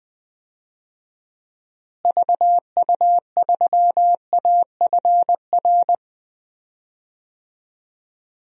Morse code, heard as a keyed tone, about 20 wpm, 700 Hz. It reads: VU3AFR
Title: DXCC-100 Call Signs - 6 Characters - Repeated 3x - 20wpm